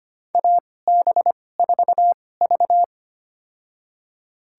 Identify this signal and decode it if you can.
Morse code, heard as a keyed tone, about 25 wpm, spaced as for 18 wpm, 700 Hz. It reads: AB4V
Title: USA Call Signs - 4 Characters - 2x Character Spacing - 25wpm